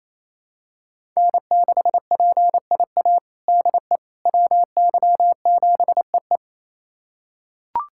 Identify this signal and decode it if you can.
Morse code, heard as a keyed tone, about 28 wpm, 700 Hz. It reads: N6PIA DE WY7EE
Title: USA Callsign DE Callsign - 5 Characters - 28wpm